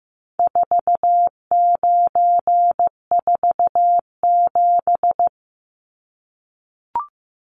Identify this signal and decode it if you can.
Morse code, heard as a keyed tone, about 15 wpm, 700 Hz. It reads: 4947
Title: Numbers - 4-Digit 15wpm